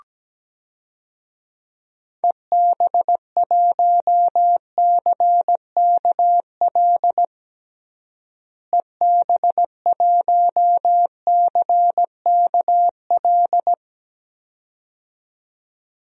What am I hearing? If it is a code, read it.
Morse code, heard as a keyed tone, about 17 wpm, 700 Hz. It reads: EB1CKL EB1CKL
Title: DXCC-100 Call Signs - 6 Characters - Repeated 2x - 17wpm